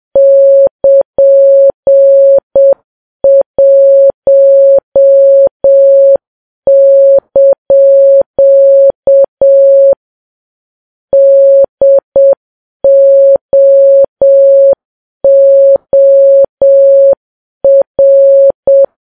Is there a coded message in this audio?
Yes: Morse code from 0.2 to 18.8 s, heard as a keyed tone, about 7 wpm, 550 Hz. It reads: <KN>1) DOOR